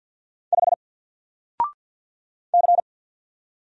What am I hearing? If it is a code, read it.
Morse code, heard as a keyed tone, about 50 wpm, 700 Hz. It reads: 5 C
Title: Single Letter-Number Alt 50wpm